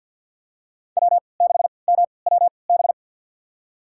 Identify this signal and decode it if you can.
Morse code, heard as a keyed tone, about 49 wpm, spaced as for 28 wpm, 700 Hz. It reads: W6MWB
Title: USA Call Signs - 5 Characters - 3x Character Spacing - 50wpm